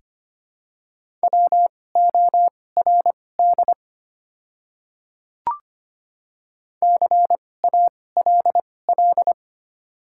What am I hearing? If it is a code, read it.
Morse code, heard as a keyed tone, about 25 wpm, spaced as for 18 wpm, 690 Hz. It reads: WORD CALL
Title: 2x Character Spacing - Top 100 Words - 25wpm